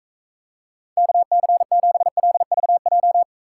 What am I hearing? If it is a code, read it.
Morse code, heard as a keyed tone, about 43 wpm, 700 Hz. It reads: KC7LVJ